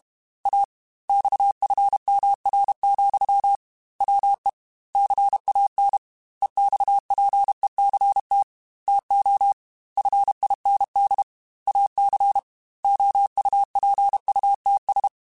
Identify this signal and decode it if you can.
Morse code, heard as a keyed tone, about 32 wpm, 800 Hz. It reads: A XFMR, WE CAN EXPECT TO FIND AC OUPUTS
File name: Technical Subjects / Technical Subjects-mp3s / Sw Pwr Sup @30